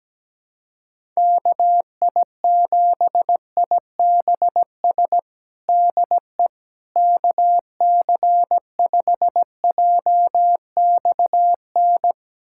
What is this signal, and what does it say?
Morse code, heard as a keyed tone, about 17 wpm, 700 Hz. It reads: KI7IBS DE KC5JXN
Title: USA Callsign DE Callsign 17wpm